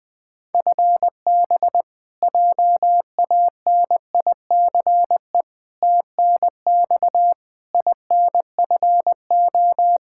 Morse code, heard as a keyed tone, about 21 wpm, 700 Hz. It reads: FB JANICE TNX INFO
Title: K1USN SST QSO Practice 20wpm